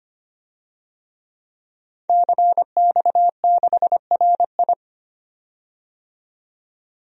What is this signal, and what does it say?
Morse code, heard as a keyed tone, about 25 wpm, 700 Hz. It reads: CX6RI
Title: DXCC-100 Call Signs - 5 Characters - 25wpm